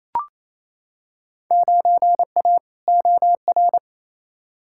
Morse code, heard as a keyed tone, about 28 wpm, 700 Hz. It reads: 9A OR
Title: Field Day Exchanges Encore - 28wpm